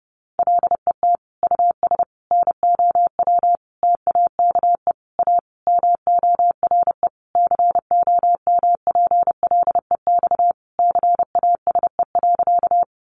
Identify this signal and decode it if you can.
Morse code, heard as a keyed tone, about 30 wpm, 700 Hz. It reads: LET US NOW TAKE A MORE COMPLEX CASE.
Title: Sentences from Top 900 Words 30wpm